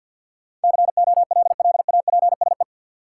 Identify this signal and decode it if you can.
Morse code, heard as a keyed tone, about 47 wpm, 700 Hz. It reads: COLLAPSE
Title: Sets of 5 Words Encore - New Word Drill - 50wpm